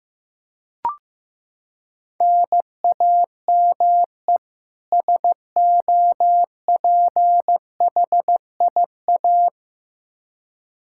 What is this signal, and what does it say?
Morse code, heard as a keyed tone, about 15 wpm, 700 Hz. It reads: NAME SOPHIA
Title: Practice QSO - Exchange 1 - Beginning - 15wpm